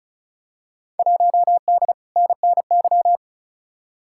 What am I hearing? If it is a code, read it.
Morse code, heard as a keyed tone, about 36 wpm, 700 Hz. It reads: 1D NNY
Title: Field Day Exchanges - 35wpm